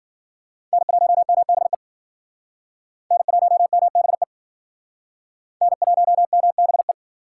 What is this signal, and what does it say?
Morse code, heard as a keyed tone, about 53 wpm, 700 Hz. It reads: N1MBE N1MBE N1MBE